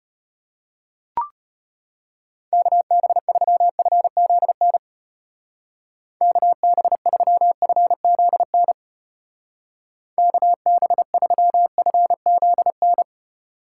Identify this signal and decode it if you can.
Morse code, heard as a keyed tone, about 31 wpm, 700 Hz. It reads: KB3FZN KB3FZN KB3FZN
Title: Speed Racing - USA 2x3 Call Signs 25wpm